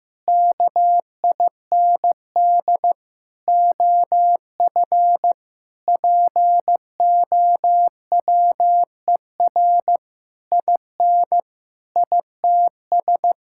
Morse code, heard as a keyed tone, about 16 wpm, 700 Hz. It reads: KIND OF POWER IN ITS
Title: Sentences from Top 1000 Words - Review Part 5 - 15wpm